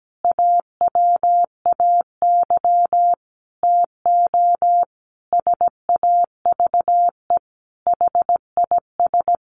Morse code, heard as a keyed tone, about 17 wpm, 700 Hz. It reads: AWAY TO SAVE HIS
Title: Sentences from Top 1100 Words 17wpm